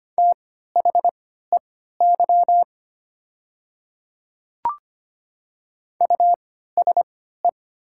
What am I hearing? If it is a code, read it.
Morse code, heard as a keyed tone, about 25 wpm, spaced as for 14 wpm, 700 Hz. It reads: THEY USE
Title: 3x Character Spacing - Top 100 Words - 25wpm